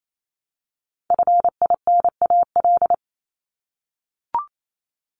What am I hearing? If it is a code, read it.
Morse code, heard as a keyed tone, about 28 wpm, 700 Hz. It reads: FINAL
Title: Top 500 Words Repeated 28wpm